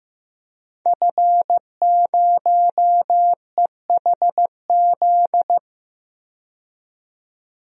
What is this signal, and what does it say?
Morse code, heard as a keyed tone, about 15 wpm, 700 Hz. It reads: F0EHZ
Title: DXCC-100 Call Signs - 5 Characters - 15wpm